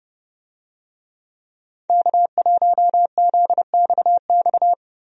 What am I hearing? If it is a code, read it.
Morse code, heard as a keyed tone, about 30 wpm, 700 Hz. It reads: K1ZXX
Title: USA Call Signs - 5 Characters - 30wpm